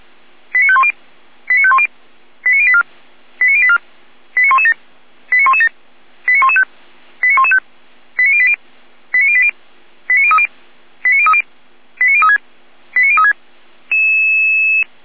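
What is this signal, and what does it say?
Five-tone selective call (ZVEI), heard as quick runs of five tones, tones of about 70 ms. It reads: 89510 89510 89084 89084 89117 89117 89115 89115 89080 89080 89030 89030 89036 89036
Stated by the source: ZVEI five-tone selective call 89120 received with an SDR.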